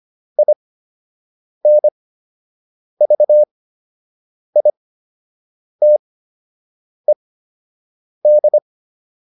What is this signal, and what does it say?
Morse code, heard as a keyed tone, about 25 wpm, spaced as for 7 wpm, 600 Hz. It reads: INVITED